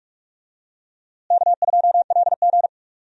Morse code, heard as a keyed tone, about 47 wpm, 700 Hz. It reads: K2LG
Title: USA Call Signs - 4 Characters - Repeated 2x - 45wpm